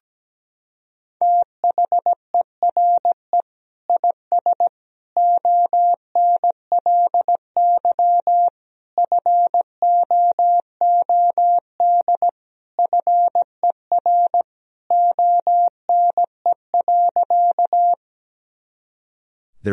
Morse code, heard as a keyed tone, about 17 wpm, 700 Hz. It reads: THERE IS ONLY FOOD FER ONE.
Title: Sentences from Top 300 Words and Abbreviations 17wpm